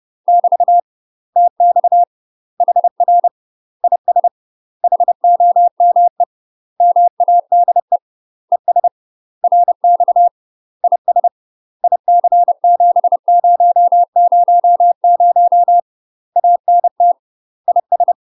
Morse code, heard as a keyed tone, about 30 wpm, 700 Hz. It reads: <BT> TX HR IS HOME MADE ES RX IS IC7000 ANT IS
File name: UK MASTER QSOs TRAINING / 30WPM / UK MASTER QSOs 0033